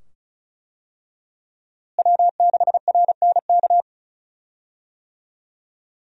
Morse code, heard as a keyed tone, about 36 wpm, 700 Hz. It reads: W6RNK
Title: USA Call Signs - 5 Characters - Repeated 2x - 35wpm